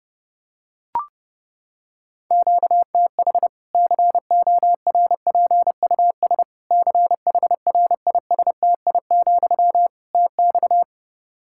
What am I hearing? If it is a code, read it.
Morse code, heard as a keyed tone, about 30 wpm, 700 Hz. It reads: QTH CORPUS CHRISTI, TX
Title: Practice QSO - Exchange 1 - Beginning - 30wpm